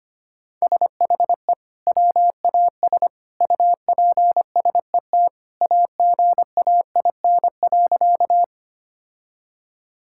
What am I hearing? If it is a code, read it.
Morse code, heard as a keyed tone, about 25 wpm, 700 Hz. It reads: SHE WAS UPSET AGAIN.
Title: Sentences from Top 1000 Words - Review Part 5 - 25wpm